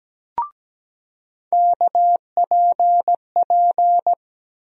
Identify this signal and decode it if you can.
Morse code, heard as a keyed tone, about 17 wpm, 700 Hz. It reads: KPP
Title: USA Call Sign Postfixes - 3 Letters 17wpm